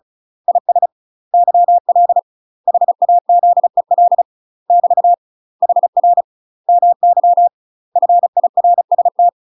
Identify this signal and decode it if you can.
Morse code, heard as a keyed tone, about 34 wpm, 700 Hz. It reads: IS YL HAZEL <BT> HR MY FIRST
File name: UK MASTER QSOs TRAINING / 35WPM / UK MASTER QSOs 0021